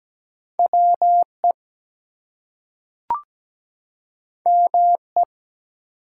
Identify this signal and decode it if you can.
Morse code, heard as a keyed tone, about 17 wpm, 700 Hz. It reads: WE ME
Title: Two Letter Words - 17wpm